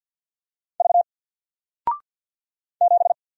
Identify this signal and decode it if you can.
Morse code, heard as a keyed tone, about 50 wpm, 700 Hz. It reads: V 7
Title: Single Letter-Number Alt 50wpm